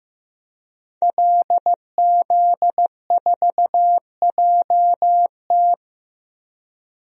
Morse code, heard as a keyed tone, about 15 wpm, 700 Hz. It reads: LZ4JT
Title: DXCC-100 Call Signs - 5 Characters - 15wpm